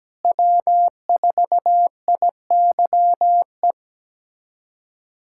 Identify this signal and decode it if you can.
Morse code, heard as a keyed tone, about 17 wpm, 700 Hz. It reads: W4IYE
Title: USA Callsign DE Callsign - 5 Characters - 17wpm